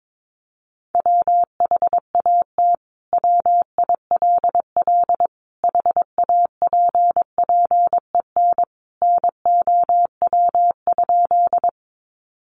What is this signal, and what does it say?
Morse code, heard as a keyed tone, about 22 wpm, 700 Hz. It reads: WHAT WILL HAPPEN NOW?
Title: Sentences from Top 400 Words 22wpm